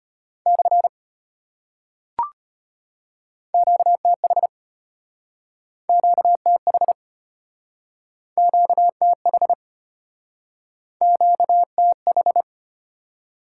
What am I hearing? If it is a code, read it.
Morse code, heard as a keyed tone, about 34 wpm, 700 Hz. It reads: / QTH QTH QTH QTH